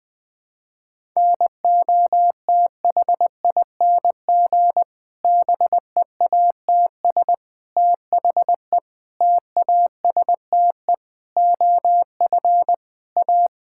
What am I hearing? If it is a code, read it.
Morse code, heard as a keyed tone, about 19 wpm, 700 Hz. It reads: NOTHING BEATS THE TASTE OF A
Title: Difficult Sentences with Hints - 20wpm